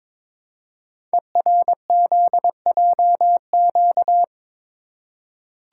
Morse code, heard as a keyed tone, about 22 wpm, 700 Hz. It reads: ERZJQ
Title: Code Groups - 5 Letters - 22wpm